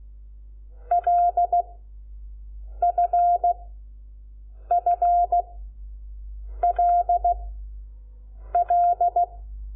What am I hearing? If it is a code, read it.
Morse code, heard as a keyed tone, about 16 wpm, spaced as for 6 wpm, 700 Hz. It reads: LFFLL
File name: Lecciones / CLASE Nº5